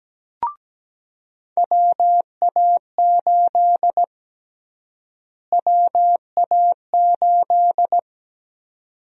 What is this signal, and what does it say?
Morse code, heard as a keyed tone, about 17 wpm, 700 Hz. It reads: WA8 WA8
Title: USA Call Sign Prefixes - Repeated 2x - 17wpm